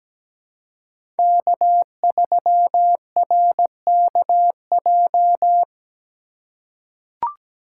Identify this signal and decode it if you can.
Morse code, heard as a keyed tone, about 17 wpm, 700 Hz. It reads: K3RKJ
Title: USA Call Signs - 5 Characters - 2x Character Spacing - 17wpm